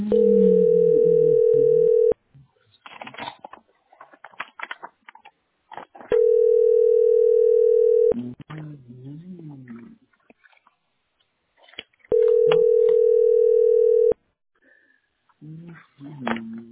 Ringback tone (North America), 440 + 480 Hz.